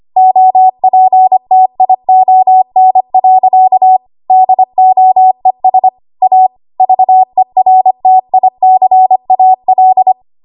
Morse code, heard as a keyed tone, about 25 wpm, 750 Hz. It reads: OPTION. DOES A VERTICAL
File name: Archive / 25 WPM / 25WPM